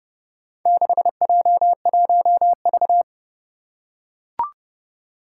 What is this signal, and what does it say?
Morse code, heard as a keyed tone, about 28 wpm, 700 Hz. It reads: 6J1V
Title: Four Challenging Letter-Number Combinations 30wpm